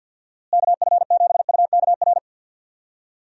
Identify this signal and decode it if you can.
Morse code, heard as a keyed tone, about 51 wpm, 700 Hz. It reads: KF7UKR
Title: USA Call Signs - 6 Characters - Repeated 2x - 50wpm